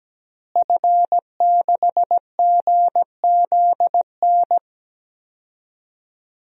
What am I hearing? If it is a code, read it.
Morse code, heard as a keyed tone, about 17 wpm, 700 Hz. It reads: F6GZN
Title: DXCC-100 Call Signs - 5 Characters - Repeated 3x - 17wpm